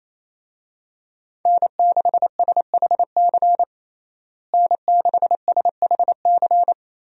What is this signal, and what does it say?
Morse code, heard as a keyed tone, about 28 wpm, 700 Hz. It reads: N6SHC N6SHC
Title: USA Call Signs - 5 Characters - Repeated 3x - 28wpm